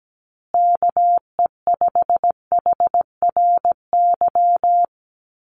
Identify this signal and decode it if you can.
Morse code, heard as a keyed tone, about 17 wpm, 700 Hz. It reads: KE5HRY
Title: USA Call Signs Repeated 2x 17wpm